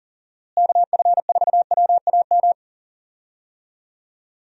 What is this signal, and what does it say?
Morse code, heard as a keyed tone, about 40 wpm, 700 Hz. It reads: KF4WAM
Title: USA Call Signs - 6 Characters - Repeated 2x - 40wpm